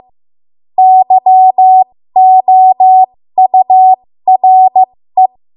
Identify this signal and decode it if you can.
Morse code, heard as a keyed tone, about 15 wpm, spaced as for 13 wpm, 750 Hz. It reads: YOURE
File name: Morse / 13WPM